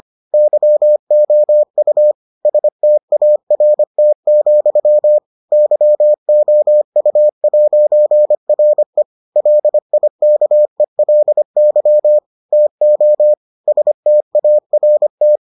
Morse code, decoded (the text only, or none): YOU START, YOU'RE LIKELY TO START